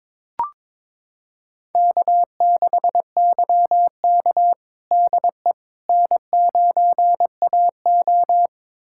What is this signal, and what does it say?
Morse code, heard as a keyed tone, about 22 wpm, 700 Hz. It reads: K6YK DE N9AO
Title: USA Callsign DE Callsign - 4 Characters - 22wpm